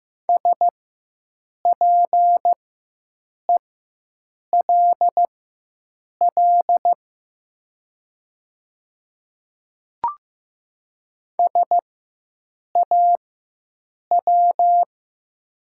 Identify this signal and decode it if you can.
Morse code, heard as a keyed tone, about 15 wpm, spaced as for 7 wpm, 700 Hz. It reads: SPELL SAW
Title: 4x Character Spacing - Top 500 Words - 15wpm